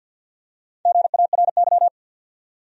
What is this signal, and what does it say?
Morse code, heard as a keyed tone, about 49 wpm, 700 Hz. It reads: GARY
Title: USA Common Names - Max 4 Letters - 50wpm